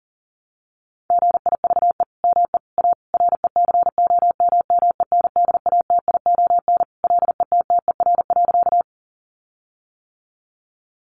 Morse code, heard as a keyed tone, about 40 wpm, 700 Hz. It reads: GIVE ME A RECOMMENDATION LETTER.
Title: Sentences from Top 1400 Words 40wpm